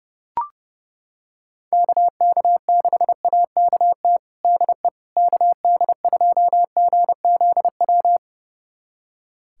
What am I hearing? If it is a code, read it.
Morse code, heard as a keyed tone, about 30 wpm, 700 Hz. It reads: KK6AKT DE KD2GZW
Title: USA Callsign DE Callsign - 6 Characters - 30wpm